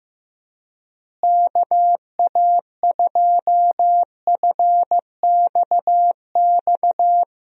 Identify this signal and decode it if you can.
Morse code, heard as a keyed tone, about 15 wpm, 700 Hz. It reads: KA2FXX
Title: USA Call Signs - 6 Characters - Repeated 3x - 15wpm